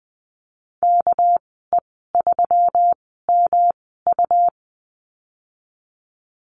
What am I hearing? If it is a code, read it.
Morse code, heard as a keyed tone, about 20 wpm, spaced as for 15 wpm, 700 Hz. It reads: KE3MU
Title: 231 Lesson 37g - Letter x - Call Signs 20wpm